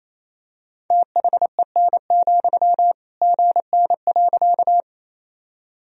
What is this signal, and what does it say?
Morse code, heard as a keyed tone, about 28 wpm, 700 Hz. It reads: THEN, GN.
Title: Sentences from Top 300 Words and Abbreviations 28wpm